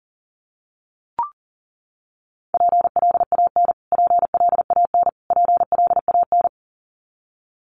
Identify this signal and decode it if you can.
Morse code, heard as a keyed tone, about 40 wpm, 700 Hz. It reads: PLAN PLAN PLAN
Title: Top 500 Words Repeated 40wpm